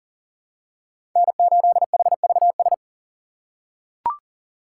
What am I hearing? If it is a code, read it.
Morse code, heard as a keyed tone, about 40 wpm, 700 Hz. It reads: N8HVS
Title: USA Call Signs - 5 Characters - 6x Character Spacing - 40wpm